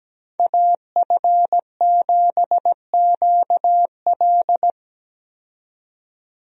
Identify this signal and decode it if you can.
Morse code, heard as a keyed tone, about 17 wpm, 700 Hz. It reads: AF7QL
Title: USA Call Signs - 5 Characters - 17wpm